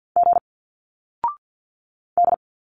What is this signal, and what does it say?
Morse code, heard as a keyed tone, about 51 wpm, 700 Hz. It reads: G D
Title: Single Letters 50wpm